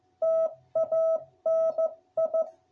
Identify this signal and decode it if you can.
Morse code, heard as a keyed tone, about 15 wpm, 660 Hz. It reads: TANI